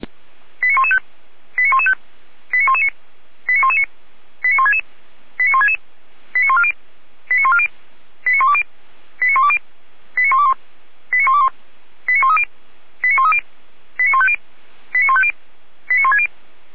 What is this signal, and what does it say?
Five-tone selective call (ZVEI), heard as quick runs of five tones, tones of about 70 ms. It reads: 89115 89115 89119 89119 89166 89166 89140 89140 89120 89120 89121 89121 89130 89130 89150 89150 89160